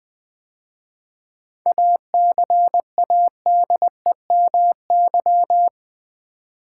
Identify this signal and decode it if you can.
Morse code, heard as a keyed tone, about 20 wpm, 700 Hz. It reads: ACADEMY
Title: 7 Letter Words - First 3 Letters Given - 20wpm